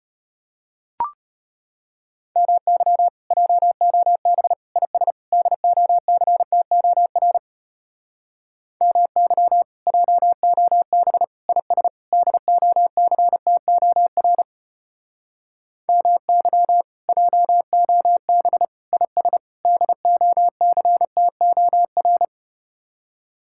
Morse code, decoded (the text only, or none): MY JOB IS DOCTOR MY JOB IS DOCTOR MY JOB IS DOCTOR